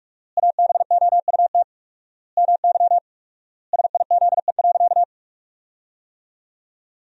Morse code, heard as a keyed tone, about 46 wpm, 700 Hz. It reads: ABOUT MY SIZE.
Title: 4x Word-Spacing - Sentences from Top 400 Words - 45wpm